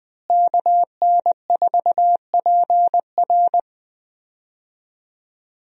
Morse code, heard as a keyed tone, about 20 wpm, 700 Hz. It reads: KN4PR
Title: USA Call Signs - 5 Characters - Repeated 2x - 20wpm